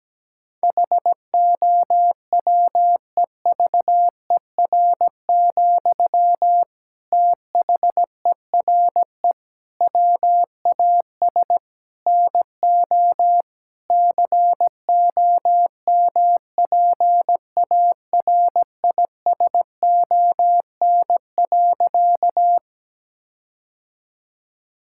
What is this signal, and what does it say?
Morse code, heard as a keyed tone, about 17 wpm, 700 Hz. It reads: HOWEVER, THERE WAS NO COMPARISON.
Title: Sentences from Top 1600 Words 17wpm